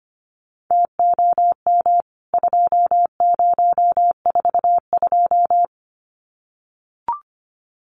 Morse code, heard as a keyed tone, about 25 wpm, 690 Hz. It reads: TOM 2042
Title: CWT Exchange 25wpm